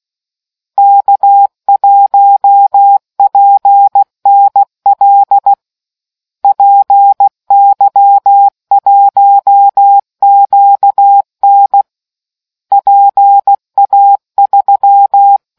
Morse code, heard as a keyed tone, about 16 wpm, 800 Hz. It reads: K1PNL PY1QN PA3